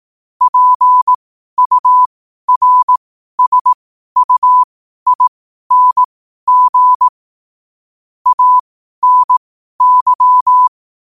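Morse code, heard as a keyed tone, about 18 wpm, spaced as for 12 wpm, 1000 Hz. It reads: PURSUING ANY
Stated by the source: Morse code quote practice at 12 words per minute.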